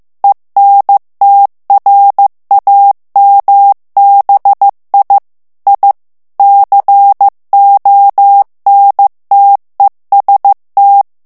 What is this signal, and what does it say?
Morse code, heard as a keyed tone, about 15 wpm, 800 Hz. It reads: ENTRAMBI I CONTEST
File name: morse mp3 / ITALIEN texte clair 2 vitesse 14 mots minute